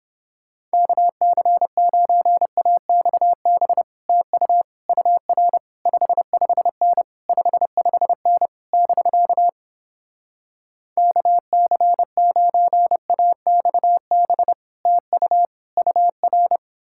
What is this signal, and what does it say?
Morse code, heard as a keyed tone, about 31 wpm, 700 Hz. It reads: KC9AXB TU UR 55N 55N <BK> KC9AXB TU UR